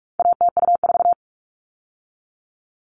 Morse code, heard as a keyed tone, about 44 wpm, 700 Hz. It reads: ATU4
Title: All Cut Numbers - 4-Digit 45wpm